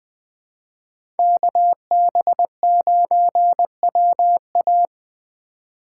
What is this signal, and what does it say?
Morse code, heard as a keyed tone, about 20 wpm, 700 Hz. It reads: KB9WA
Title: Speed Racing - USA 2x2 Call Signs 17wpm